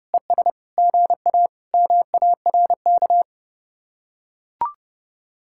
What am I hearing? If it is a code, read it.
Morse code, heard as a keyed tone, about 32 wpm, 700 Hz. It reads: ES GA MARK